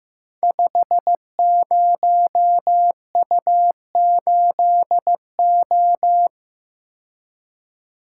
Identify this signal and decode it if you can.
Morse code, heard as a keyed tone, about 15 wpm, 700 Hz. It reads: 50U8O
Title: Code Groups - 5 Letter:Numbers - 15wpm